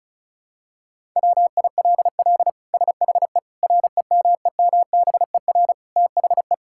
Morse code, heard as a keyed tone, about 36 wpm, 700 Hz. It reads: WILL SHE REMEMBER THE
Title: Sentences from Top 1000 Words - Review Part 6 - 35wpm